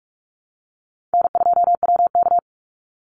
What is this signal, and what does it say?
Morse code, heard as a keyed tone, about 44 wpm, 700 Hz. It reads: N2WK
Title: Call Signs - 45wpm